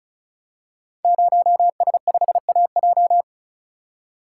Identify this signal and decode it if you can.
Morse code, heard as a keyed tone, about 36 wpm, 690 Hz. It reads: 0S5AJ
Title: Code Groups - 5 Letter:Numbers - 35wpm